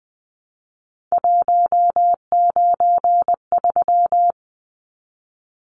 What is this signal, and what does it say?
Morse code, heard as a keyed tone, about 20 wpm, 700 Hz. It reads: 193
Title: Numbers - 3-Digit 20wpm